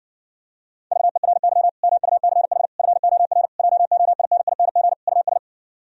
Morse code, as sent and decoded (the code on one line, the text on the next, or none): ...- . .-. -.-- / -- ..- -.-. .... / ..-. --- .-. / .--- --- .. -. .. -. --. / ..- ...
VERY MUCH FOR JOINING US